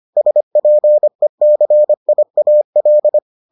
Morse code, heard as a keyed tone, about 25 wpm, 600 Hz. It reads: SPECIAL